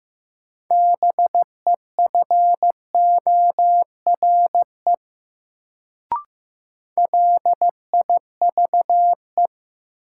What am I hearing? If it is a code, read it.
Morse code, heard as a keyed tone, about 15 wpm, 700 Hz. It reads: BEFORE LIVE
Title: Sentences from Top 200 Words - New Word Drill - 15wpm